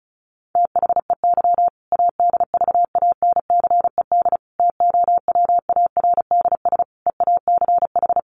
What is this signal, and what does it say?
Morse code, heard as a keyed tone, about 34 wpm, 700 Hz. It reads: THEY ADVANCED TOWARDS EACH
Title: Sentences from Top 1200 Words 35wpm